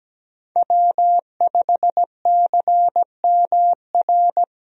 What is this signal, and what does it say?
Morse code, heard as a keyed tone, about 17 wpm, 700 Hz. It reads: W5CMR
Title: USA Call Signs - 5 Characters - Repeated 3x - 17wpm